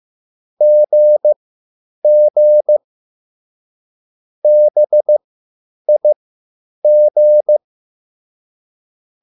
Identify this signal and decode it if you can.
Morse code, heard as a keyed tone, about 15 wpm, spaced as for 9 wpm, 600 Hz. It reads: GG BIG